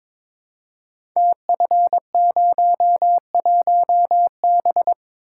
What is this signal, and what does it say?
Morse code, heard as a keyed tone, about 22 wpm, 700 Hz. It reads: TF01B
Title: Code Groups - 5 Letter:Numbers - 22wpm